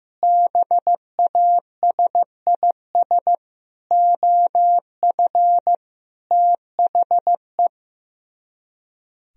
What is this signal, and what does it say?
Morse code, heard as a keyed tone, about 15 wpm, 700 Hz. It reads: BASIS OF THE
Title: Speed-Racing - Sets of 5 Words 15wpm